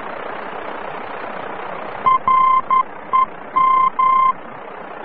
Morse code, heard as a keyed tone, about 11 wpm, 1040 Hz. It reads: REM
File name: france / mx / REM-VOR-F